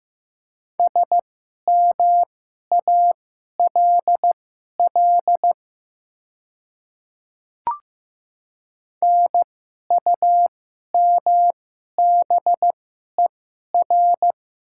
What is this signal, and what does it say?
Morse code, heard as a keyed tone, about 15 wpm, spaced as for 11 wpm, 700 Hz. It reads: SMALL NUMBER
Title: 2x Character Spacing - Top 200 Words - 15wpm